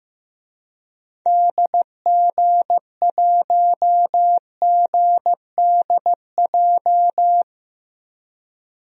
Morse code, heard as a keyed tone, about 15 wpm, 700 Hz. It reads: DG1GDJ
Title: DXCC-100 Call Signs - 6 Characters - 15wpm